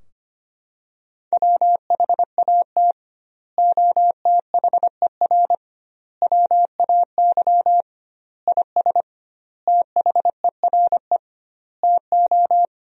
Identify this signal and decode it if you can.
Morse code, heard as a keyed tone, about 25 wpm, 700 Hz. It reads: WHAT OTHER WAY IS THERE TO
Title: 2x Word-Spacing - Sentences from Top 100 Words - 25wpm